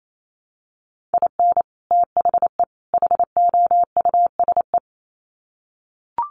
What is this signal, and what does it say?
Morse code, heard as a keyed tone, about 28 wpm, 700 Hz. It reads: IN THE HOUSE
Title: Sets of 3 Words 28wpm